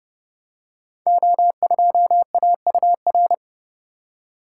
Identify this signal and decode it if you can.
Morse code, heard as a keyed tone, about 30 wpm, 700 Hz. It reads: O2AUR
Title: Code Groups - 5 Letter:Numbers - 30wpm